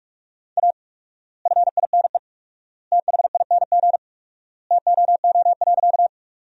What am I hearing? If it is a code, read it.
Morse code, heard as a keyed tone, about 44 wpm, 700 Hz. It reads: A FINE THING TOO.
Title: 4x Word-Spacing - Sentences from Top 500 Words - 45wpm